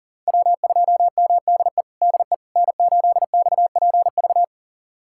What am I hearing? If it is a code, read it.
Morse code, heard as a keyed tone, about 40 wpm, 700 Hz. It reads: W2MDE DE N8XPV
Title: USA Callsign DE Callsign - 5 Characters - 40wpm